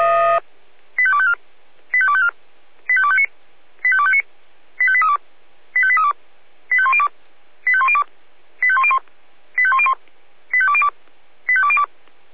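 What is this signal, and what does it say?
Five-tone selective call (ZVEI), heard as quick runs of five tones, tones of about 70 ms. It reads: 85254 85254 85259 85259 85832 85832 85102 85102 85101 85101 85202 85202
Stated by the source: ZVEI five-tone call received on SDR.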